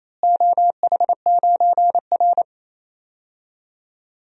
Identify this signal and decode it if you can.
Morse code, heard as a keyed tone, about 28 wpm, 700 Hz. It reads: OH9R
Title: DXCC-100 Call Signs - 4 Characters - Repeated 3x - 28wpm